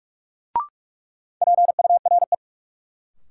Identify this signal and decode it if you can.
Morse code, heard as a keyed tone, about 46 wpm, 700 Hz. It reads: PURE
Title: Sentences from Top 1500 Words - New Word Drill - 45wpm